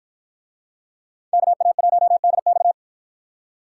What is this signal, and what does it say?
Morse code, heard as a keyed tone, about 51 wpm, 700 Hz. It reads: KA1DX